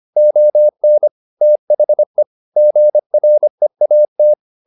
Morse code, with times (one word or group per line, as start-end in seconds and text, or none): 0.2-1.1 ON
1.4-2.2 THE
2.6-4.3 GREAT